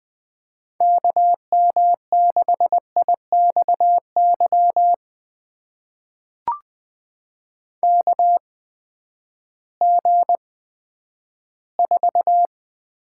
Morse code, heard as a keyed tone, about 20 wpm, spaced as for 11 wpm, 700 Hz. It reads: KM6IXY KG4
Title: USA Call Signs - 6 Characters - 8x Character Spacing - 20wpm